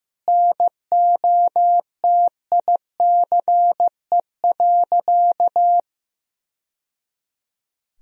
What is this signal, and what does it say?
Morse code, heard as a keyed tone, about 15 wpm, 700 Hz. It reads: NOTICE.
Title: Sentences from Top 1000 Words - Review Part 2 - 15wpm